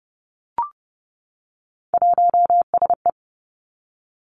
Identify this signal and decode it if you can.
Morse code, heard as a keyed tone, about 30 wpm, 700 Hz. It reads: 1SE
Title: Three Challenging Letter-Number Combinations 30wpm